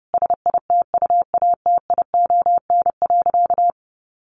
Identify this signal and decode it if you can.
Morse code, heard as a keyed tone, about 28 wpm, 700 Hz. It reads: SITUATION.
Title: Sentences from Top 1200 Words 30wpm